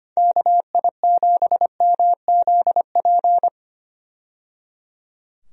Morse code, heard as a keyed tone, about 25 wpm, 700 Hz. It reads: KI7MZP
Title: USA Call Signs - 6 Characters - 25wpm